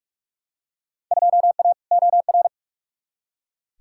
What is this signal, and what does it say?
Morse code, heard as a keyed tone, about 43 wpm, 700 Hz. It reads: 2A OR
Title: Field Day Exchanges - 45wpm